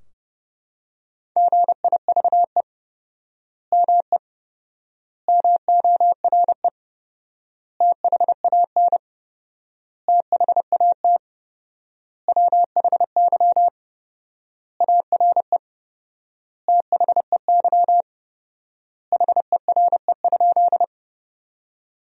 Morse code, heard as a keyed tone, about 30 wpm, 700 Hz. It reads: GIVE ME MORE THAN THAT WHY ARE THEY HERE?
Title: 4x Word-Spacing - Sentences from Top 200 Words - 30wpm